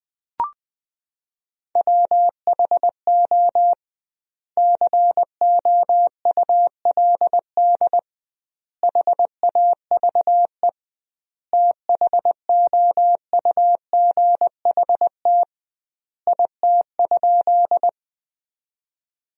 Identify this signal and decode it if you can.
Morse code, heard as a keyed tone, about 20 wpm, 700 Hz. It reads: WHO COULD HAVE THOUGHT IT?
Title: 2x Word-Spacing - Sentences from Top 300 Words - 20wpm